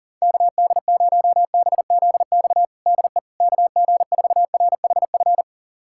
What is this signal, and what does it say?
Morse code, heard as a keyed tone, about 40 wpm, 700 Hz. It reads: KD0BZX DE KG4RHF
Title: USA Callsign DE Callsign - 6 Characters - 40wpm